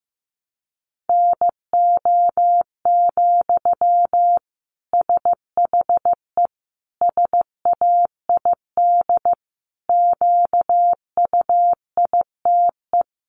Morse code, heard as a keyed tone, about 15 wpm, 700 Hz. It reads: NO, SHE SAID QUITE
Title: Sentences from Top 800 Words 15wpm